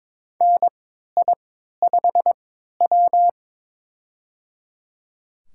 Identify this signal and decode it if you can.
Morse code, heard as a keyed tone, about 22 wpm, spaced as for 13 wpm, 700 Hz. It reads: NI5W
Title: USA Call Signs - 4 Characters - 3x Character Spacing - 22wpm